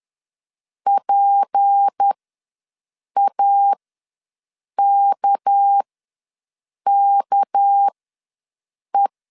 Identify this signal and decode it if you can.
Morse code, heard as a keyed tone, about 10 wpm, spaced as for 6 wpm, 800 Hz. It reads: PAKKE
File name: LEZIONI / 6 PKX0 / 6 PK ws 10wpm 3745